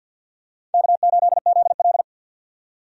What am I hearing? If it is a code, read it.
Morse code, heard as a keyed tone, about 52 wpm, 700 Hz. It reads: K8ZL
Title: USA Call Signs - 4 Characters - 50wpm